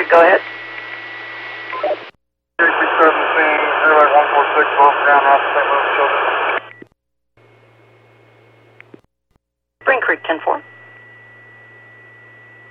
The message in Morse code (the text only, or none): none